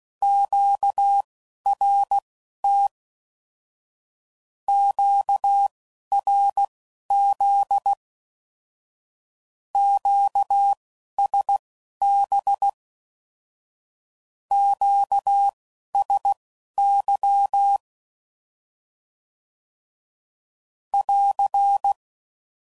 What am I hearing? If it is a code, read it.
Morse code, heard as a keyed tone, about 16 wpm, spaced as for 11 wpm, 800 Hz. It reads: QRT QRZ QSB QSY <AR>
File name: CW Code Groups @10WPM / Q Calls / Q Calls Once